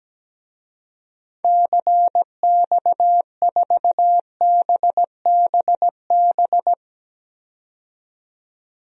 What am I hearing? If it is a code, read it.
Morse code, heard as a keyed tone, about 17 wpm, 700 Hz. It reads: CX4BBB
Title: DXCC-100 Call Signs - 6 Characters - 17wpm